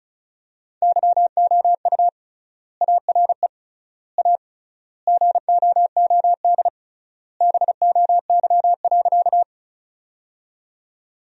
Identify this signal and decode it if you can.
Morse code, heard as a keyed tone, about 34 wpm, 700 Hz. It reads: YOU ARE A GOOD BOY.
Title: 3x Word-Spacing - Sentences from Top 200 Words - 35wpm